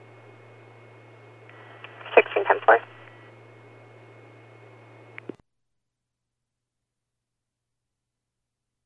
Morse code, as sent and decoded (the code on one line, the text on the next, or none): none